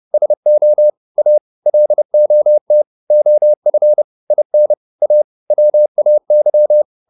Morse code, heard as a keyed tone, about 30 wpm, 600 Hz. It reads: SO A LOT OF IN A WAY